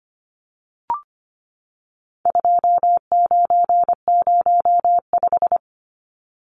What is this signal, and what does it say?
Morse code, heard as a keyed tone, about 25 wpm, 700 Hz. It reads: 2905
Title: Numbers - 4-Digit 25wpm